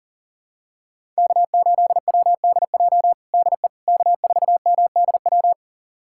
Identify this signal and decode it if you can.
Morse code, heard as a keyed tone, about 40 wpm, 700 Hz. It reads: K8WDJ DE K4MDW
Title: USA Callsign DE Callsign - 5 Characters - 40wpm